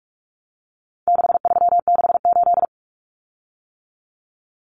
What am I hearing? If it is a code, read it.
Morse code, heard as a keyed tone, about 44 wpm, 700 Hz. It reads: 6368
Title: Common Cut Numbers - 4-Digit 45wpm